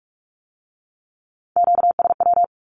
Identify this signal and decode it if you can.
Morse code, heard as a keyed tone, about 46 wpm, 700 Hz. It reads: QSW
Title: Q-Codes with Spoken Meaning - 45wpm